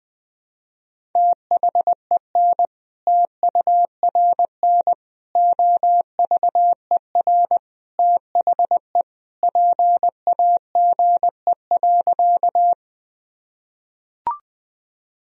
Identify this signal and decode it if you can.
Morse code, heard as a keyed tone, about 20 wpm, 700 Hz. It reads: THEN TURN OVER THE PAGE.
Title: Sentences from Top 1000 Words - Review Part 1 - 20wpm